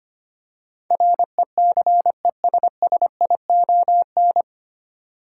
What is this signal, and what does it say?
Morse code, heard as a keyed tone, about 25 wpm, 700 Hz. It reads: RECESSION
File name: practice / 25wpm / 9 Letter Words - First 3 Letters Given - 25wpm